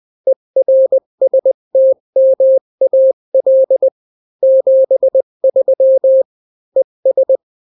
Morse code, heard as a keyed tone, about 20 wpm, 530 Hz. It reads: ERSTMAL 73 ES
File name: daily / 18wpm